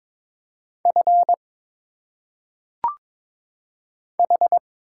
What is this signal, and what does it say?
Morse code, heard as a keyed tone, about 22 wpm, 700 Hz. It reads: F H
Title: Single Letters - Phonetic Pronunciation 22wpm